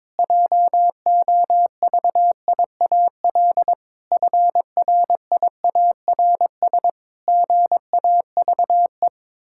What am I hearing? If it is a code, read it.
Morse code, heard as a keyed tone, about 22 wpm, 700 Hz. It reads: JOVIAL FRIARS GAVE
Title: Pangram Challenge v2 - 22wpm